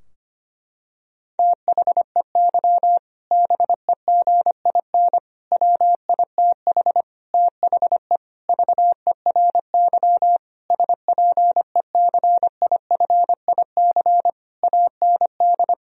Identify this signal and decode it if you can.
Morse code, heard as a keyed tone, about 25 wpm, 700 Hz. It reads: THEY BEGIN WITH THE VERY SPECIFIC AND
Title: Sentences from Top 1000 Words - Review Part 3 - 25wpm